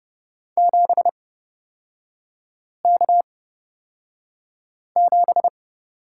Morse code, heard as a keyed tone, about 30 wpm, spaced as for 5 wpm, 700 Hz. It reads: 7K7
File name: Lesson 36e - BT - Three Characters 30wpm